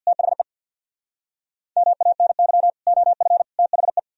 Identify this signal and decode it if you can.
Morse code, heard as a keyed tone, about 64 wpm, 700 Hz. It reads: THE MANY OF THE